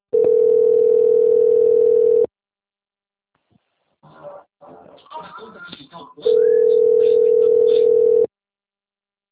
Ringback tone (North America), 440 + 480 Hz.